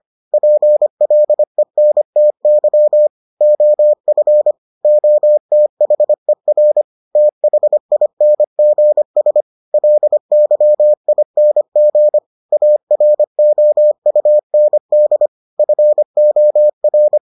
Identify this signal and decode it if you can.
Morse code, heard as a keyed tone, about 25 wpm, 600 Hz. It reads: PLENTY OF OTHER THINGS LYING AROUND FOR